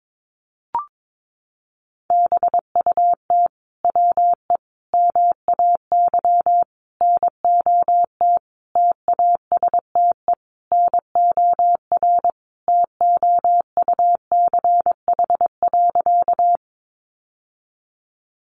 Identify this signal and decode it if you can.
Morse code, heard as a keyed tone, about 22 wpm, 700 Hz. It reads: BUT WE MAY NOT TASTE NOR TOUCH.
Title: Sentences from Top 1500 Words 22wpm